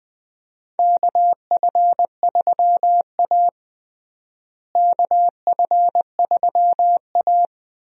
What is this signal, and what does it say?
Morse code, heard as a keyed tone, about 21 wpm, 700 Hz. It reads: KF3A KF3A
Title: USA Call Signs - 4 Characters - Repeated 2x - 20wpm